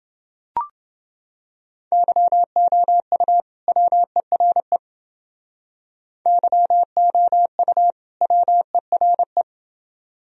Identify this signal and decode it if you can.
Morse code, heard as a keyed tone, about 29 wpm, 700 Hz. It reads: YOU WERE YOU WERE